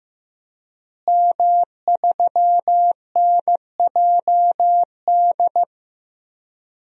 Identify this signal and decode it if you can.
Morse code, heard as a keyed tone, about 15 wpm, 700 Hz. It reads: M3NJD
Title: DXCC-100 Call Signs - 5 Characters - Repeated 2x - 15wpm